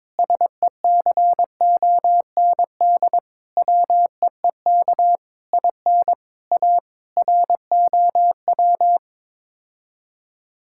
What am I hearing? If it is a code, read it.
Morse code, heard as a keyed tone, about 21 wpm, 700 Hz. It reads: SECOND WEEK IN A ROW
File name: practice / 22wpm / Sets of 7 Words - 22wpm